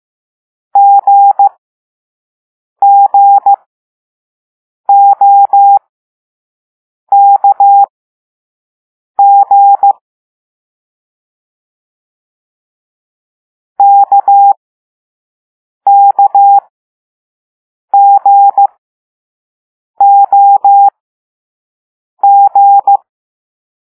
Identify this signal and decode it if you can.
Morse code, heard as a keyed tone, about 15 wpm, spaced as for 5 wpm, 800 Hz. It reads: GGOKG KKGOG
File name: Morse / Level4B